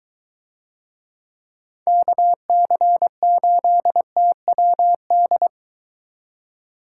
Morse code, heard as a keyed tone, about 23 wpm, 700 Hz. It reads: KC8TWD